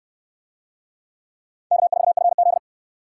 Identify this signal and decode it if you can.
Morse code, heard as a keyed tone, about 58 wpm, 700 Hz. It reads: K4FZ